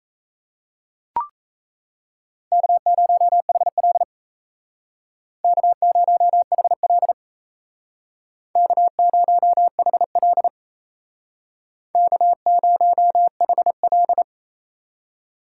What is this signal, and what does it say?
Morse code, heard as a keyed tone, about 35 wpm, 700 Hz. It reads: K0HL K0HL K0HL K0HL